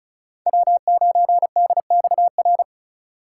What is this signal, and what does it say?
Morse code, heard as a keyed tone, about 36 wpm, 700 Hz. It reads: W9DXR
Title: K1USN SST QSO Practice - 35wpm